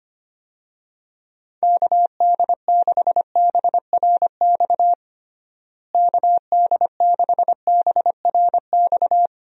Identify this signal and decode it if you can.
Morse code, heard as a keyed tone, about 25 wpm, 700 Hz. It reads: KD6BRX KD6BRX
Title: USA Call Signs - 6 Characters - Repeated 3x - 25wpm